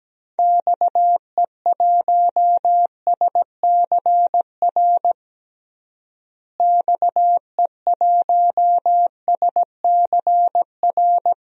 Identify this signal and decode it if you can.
Morse code, heard as a keyed tone, about 17 wpm, 700 Hz. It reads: XE1SCR XE1SCR
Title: DXCC-100 Call Signs - 6 Characters - Repeated 3x - 17wpm